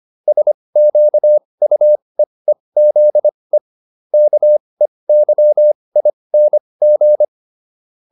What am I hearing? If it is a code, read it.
Morse code, heard as a keyed tone, about 25 wpm, spaced as for 20 wpm, 600 Hz. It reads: SQUEEZE KEYING
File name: SS315 20